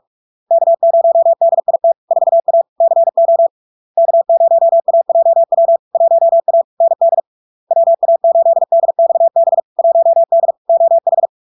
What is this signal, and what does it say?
Morse code, heard as a keyed tone, about 46 wpm, 670 Hz. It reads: K0DIT 4A CO K0AJW 1A ND WA8DXB 1D OH
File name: LARK-FD-2022-Log-in-CW-45wpm-670-Hz